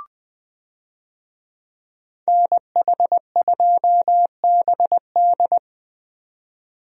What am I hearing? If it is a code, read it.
Morse code, heard as a keyed tone, about 20 wpm, 700 Hz. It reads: NH2BD
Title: USA Call Signs - 5 Characters - 20wpm